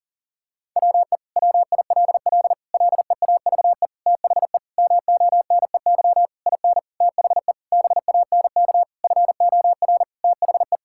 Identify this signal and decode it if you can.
Morse code, heard as a keyed tone, about 40 wpm, 700 Hz. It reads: WE WILL LEAVE THE MONEY IN THE BANK FOR THE
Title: Sentences from Top 1000 Words - Review Part 6 - 40wpm